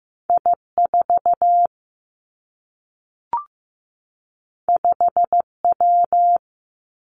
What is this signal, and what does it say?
Morse code, heard as a keyed tone, about 15 wpm, 700 Hz. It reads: I 4 5 W
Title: Two Challenging Letter-Number Combinations 15wpm